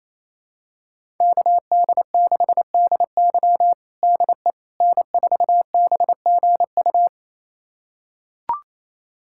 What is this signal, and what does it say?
Morse code, heard as a keyed tone, about 28 wpm, 700 Hz. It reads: KD6DY DE N4BGU
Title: USA Callsign DE Callsign - 5 Characters - 28wpm